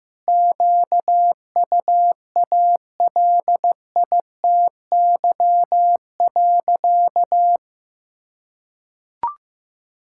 Morse code, heard as a keyed tone, about 15 wpm, 700 Hz. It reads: QUALITY.
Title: Sentences from Top 1000 Words - Review Part 5 - 15wpm